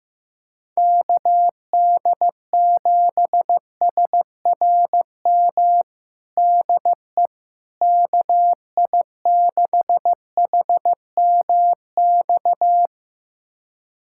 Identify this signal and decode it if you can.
Morse code, heard as a keyed tone, about 15 wpm, 700 Hz. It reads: KD7SRM DE KI6HMX
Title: USA Callsign DE Callsign 15wpm